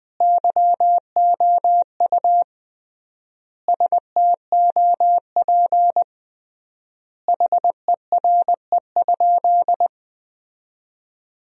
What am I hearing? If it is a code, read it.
Morse code, heard as a keyed tone, about 20 wpm, 700 Hz. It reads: YOU STOP HERE?
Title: 3x Word-Spacing - Sentences from Top 300 Words - 20wpm